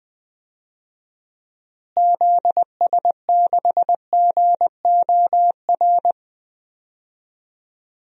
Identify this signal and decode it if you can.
Morse code, heard as a keyed tone, about 20 wpm, 700 Hz. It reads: ZS6GOR
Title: DXCC-100 Call Signs - 6 Characters - Repeated 3x - 20wpm